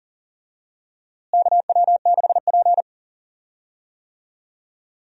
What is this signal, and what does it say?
Morse code, heard as a keyed tone, about 40 wpm, 700 Hz. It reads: KW6P
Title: USA Call Signs - 4 Characters - Repeated 2x - 40wpm